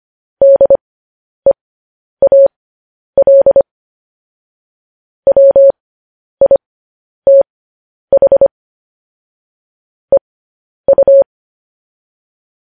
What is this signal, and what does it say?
Morse code, heard as a keyed tone, about 25 wpm, spaced as for 10 wpm, 550 Hz. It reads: DEAL WITH EU
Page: news headlines in morse code at 10 wpm